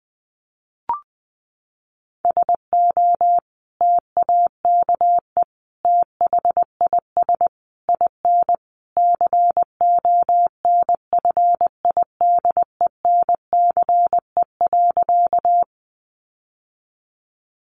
Morse code, heard as a keyed tone, about 20 wpm, 700 Hz. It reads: SO TAKE THIS IN CONFIDENCE.
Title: Sentences from Top 1500 Words 20wpm